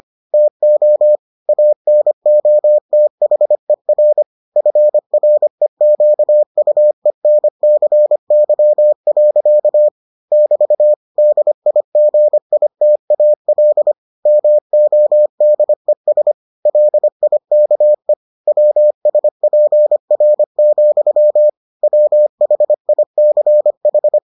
Morse code, heard as a keyed tone, about 25 wpm, 600 Hz. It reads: TO ANOTHER FREQUENCY. <BT> DIGITAL MODES LIKE WSPR, WHICH